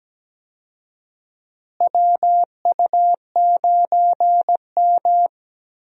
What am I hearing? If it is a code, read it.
Morse code, heard as a keyed tone, about 17 wpm, 700 Hz. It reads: WU9M
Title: USA Call Signs - 4 Characters - Repeated 2x - 17wpm